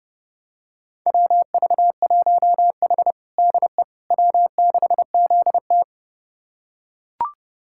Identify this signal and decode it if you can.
Morse code, heard as a keyed tone, about 30 wpm, 700 Hz. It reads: WV1H DE W6ZT
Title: USA Callsign DE Callsign - 4 Characters - 30wpm